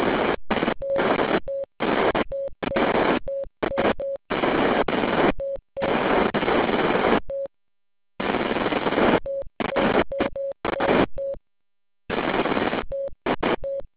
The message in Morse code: none